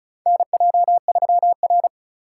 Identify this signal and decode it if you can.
Morse code, heard as a keyed tone, about 36 wpm, 690 Hz. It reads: NJ3R
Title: USA Call Signs - 4 Characters - 3x Character Spacing - 35wpm